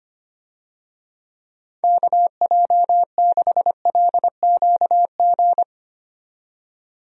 Morse code, heard as a keyed tone, about 25 wpm, 700 Hz. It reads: KJ6LQG
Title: USA Call Signs - 6 Characters - 25wpm